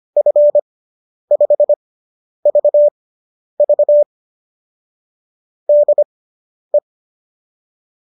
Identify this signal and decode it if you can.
Morse code, heard as a keyed tone, about 25 wpm, spaced as for 10 wpm, 590 Hz. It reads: F5VV DE